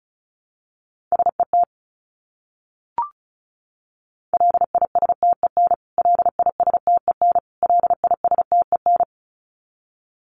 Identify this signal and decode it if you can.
Morse code, heard as a keyed tone, about 34 wpm, 700 Hz. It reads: SET LISTEN LISTEN LISTEN
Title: Top 500 Words Repeated 35wpm